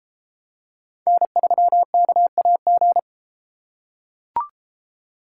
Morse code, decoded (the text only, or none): N3KAG